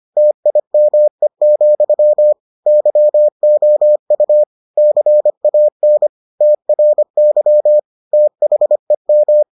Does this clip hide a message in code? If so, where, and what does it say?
Yes: Morse code from 0.2 to 9.4 s, heard as a keyed tone, about 25 wpm, 600 Hz. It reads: TIME, YOU CAN TRY THEM